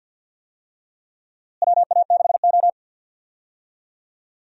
Morse code, heard as a keyed tone, about 47 wpm, 700 Hz. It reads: WA6O
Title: USA Call Signs - 4 Characters - Repeated 2x - 50wpm